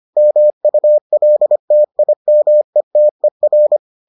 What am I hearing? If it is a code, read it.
Morse code, heard as a keyed tone, about 25 wpm, 600 Hz. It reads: MULTIMETER